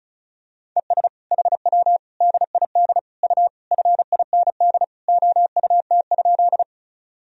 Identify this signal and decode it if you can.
Morse code, heard as a keyed tone, about 34 wpm, 700 Hz. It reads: ES HW DID U FIND OUT?
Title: Sentences from Top 100 Words and Abbreviations 35wpm